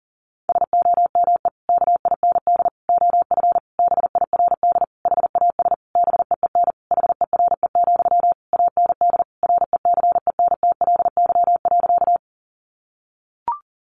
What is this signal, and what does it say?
Morse code, heard as a keyed tone, about 37 wpm, 700 Hz. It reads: SOME KIND OF BIRD HAS BEEN HERE, AND RECENTLY.
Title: Sentences from Top 1200 Words 40wpm